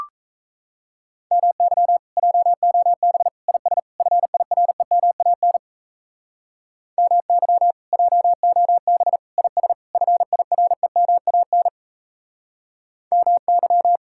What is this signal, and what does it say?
Morse code, heard as a keyed tone, about 38 wpm, 700 Hz. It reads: MY JOB IS FIREMAN MY JOB IS FIREMAN MY